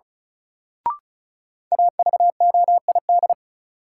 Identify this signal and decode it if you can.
Morse code, heard as a keyed tone, about 36 wpm, 700 Hz. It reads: AVOID
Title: Sentences from Top 1200 Words - New Word Drill Alt - 35wpm